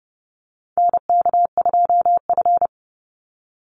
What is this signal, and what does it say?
Morse code, heard as a keyed tone, about 30 wpm, 700 Hz. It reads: NK2F
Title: Call Signs Repeated 3x - 30wpm